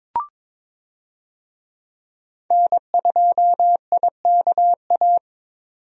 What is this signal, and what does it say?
Morse code, heard as a keyed tone, about 22 wpm, 700 Hz. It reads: N2IKA
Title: USA Call Signs - 5 Characters - 22wpm